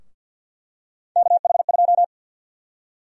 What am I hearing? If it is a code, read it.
Morse code, heard as a keyed tone, about 53 wpm, 700 Hz. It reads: KH2
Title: USA Call Sign Prefixes - 50wpm